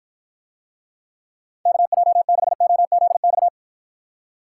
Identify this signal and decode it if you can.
Morse code, heard as a keyed tone, about 54 wpm, 700 Hz. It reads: KJ6OZX